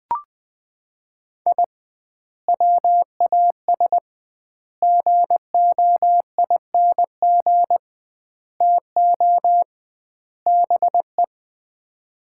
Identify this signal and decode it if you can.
Morse code, heard as a keyed tone, about 21 wpm, 700 Hz. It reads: I WAS GOING TO BE
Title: 2x Word-Spacing - Sets of 5 Words - 20wpm